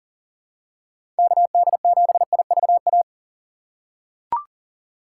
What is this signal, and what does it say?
Morse code, heard as a keyed tone, about 40 wpm, 700 Hz. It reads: KD7IVA
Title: USA Call Signs - 4x Character Spacing - 40wpm